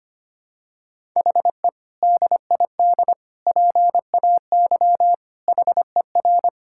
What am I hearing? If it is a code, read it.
Morse code, heard as a keyed tone, about 25 wpm, 700 Hz. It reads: HE DID PAY HER
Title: Sentences from Top 1000 Words - Review Part 6 - 25wpm